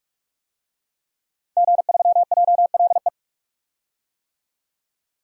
Morse code, heard as a keyed tone, about 43 wpm, 700 Hz. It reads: G3JLE
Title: DXCC-100 Call Signs - 5 Characters - 45wpm